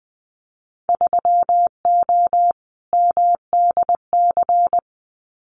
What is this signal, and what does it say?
Morse code, heard as a keyed tone, about 19 wpm, 700 Hz. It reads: 3O MDC
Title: Winter Field Day Exchanges 20wpm